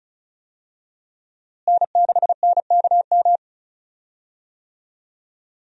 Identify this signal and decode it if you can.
Morse code, heard as a keyed tone, about 34 wpm, 700 Hz. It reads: N6NKM
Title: USA Call Signs - 5 Characters - Repeated 2x - 35wpm